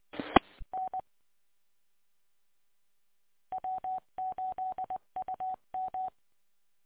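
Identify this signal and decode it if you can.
Morse code, heard as a keyed tone, about 20 wpm, 750 Hz. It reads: E N W8UM
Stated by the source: Recording archive of W8UM station, 145.230 MHz.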